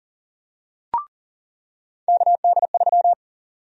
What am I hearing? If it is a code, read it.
Morse code, heard as a keyed tone, about 40 wpm, 700 Hz. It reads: KD3
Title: USA Call Sign Prefixes - 40wpm